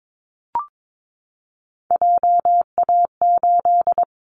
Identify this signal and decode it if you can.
Morse code, heard as a keyed tone, about 22 wpm, 700 Hz. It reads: JA8
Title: Callsign Prefixes for Top 50 DXCC - 22wpm